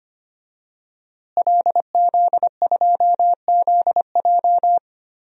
Morse code, heard as a keyed tone, about 25 wpm, 700 Hz. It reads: LZ2ZJ
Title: DXCC-100 Call Signs - 5 Characters - Repeated 2x - 25wpm